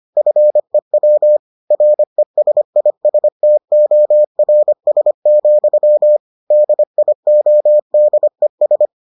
Morse code, heard as a keyed tone, about 25 wpm, 600 Hz. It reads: FEW RESISTORS, DIODES